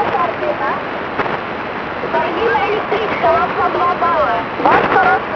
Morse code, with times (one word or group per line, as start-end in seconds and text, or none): none